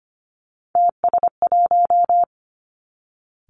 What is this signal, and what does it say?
Morse code, heard as a keyed tone, about 25 wpm, 700 Hz. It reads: TS1
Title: Three Challenging Letter-Number Combinations 25wpm